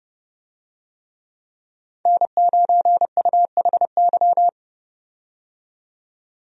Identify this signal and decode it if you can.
Morse code, heard as a keyed tone, about 32 wpm, 700 Hz. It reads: N9UHY
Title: USA Call Signs - 5 Characters - 30wpm